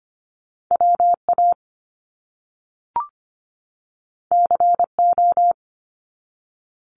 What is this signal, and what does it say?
Morse code, heard as a keyed tone, about 25 wpm, 700 Hz. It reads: WA CO
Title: US States 25wpm